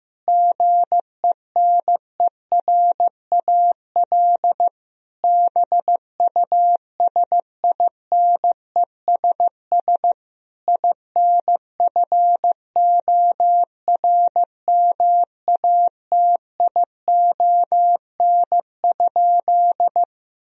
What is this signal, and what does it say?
Morse code, heard as a keyed tone, about 16 wpm, 700 Hz. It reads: GENERAL BUSINESS INFORMATION?
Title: Sentences from Top 1000 Words - Review Part 3 - 15wpm